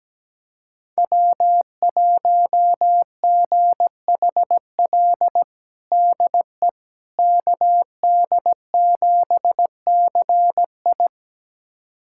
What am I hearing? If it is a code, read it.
Morse code, heard as a keyed tone, about 17 wpm, 700 Hz. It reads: W1GHL DE KD7CI
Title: USA Callsign DE Callsign - 5 Characters - 17wpm